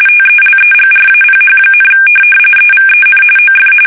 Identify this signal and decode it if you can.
RTTY, about 45.45 baud. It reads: TELEGRAPH, EVEN OF